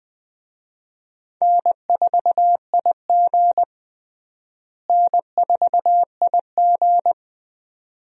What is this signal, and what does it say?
Morse code, heard as a keyed tone, about 20 wpm, 700 Hz. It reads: N4IG N4IG
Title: USA Call Signs - 4 Characters - Repeated 2x - 20wpm